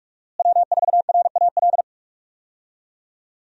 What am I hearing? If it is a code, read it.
Morse code, heard as a keyed tone, about 44 wpm, 700 Hz. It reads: W4RAL
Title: USA Call Signs - 5 Characters - Repeated 2x - 45wpm